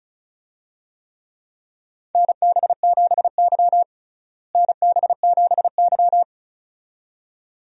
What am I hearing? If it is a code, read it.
Morse code, heard as a keyed tone, about 36 wpm, 700 Hz. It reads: NB7Y NB7Y
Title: USA Call Signs - 4 Characters - Repeated 2x - 35wpm